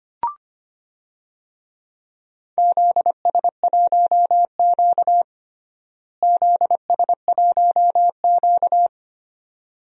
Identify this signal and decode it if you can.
Morse code, heard as a keyed tone, about 25 wpm, 700 Hz. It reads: ZS1Q ZS1Q
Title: DXCC-100 Call Signs - 4 Characters - Repeated 2x - 25wpm